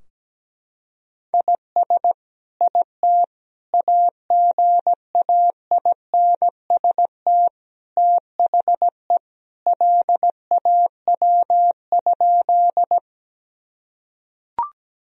Morse code, heard as a keyed tone, about 17 wpm, 700 Hz. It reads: IS IT AGAINST THE LAW?
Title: Sentences from Top 1000 Words - Review Part 3 - 17wpm